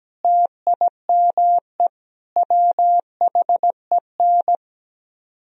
Morse code, heard as a keyed tone, about 17 wpm, 700 Hz. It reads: TIME WHEN
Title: Speed-Racing - Sets of 5 Words 17wpm